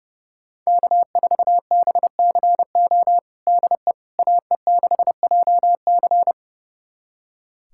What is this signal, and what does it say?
Morse code, heard as a keyed tone, about 30 wpm, 700 Hz. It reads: K4BCO DE AE6JC
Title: USA Callsign DE Callsign - 5 Characters - 30wpm